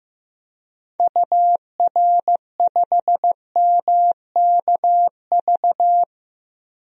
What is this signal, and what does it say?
Morse code, heard as a keyed tone, about 15 wpm, 700 Hz. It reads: UR5MKV
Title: DXCC-100 Call Signs - 6 Characters - Repeated 2x - 15wpm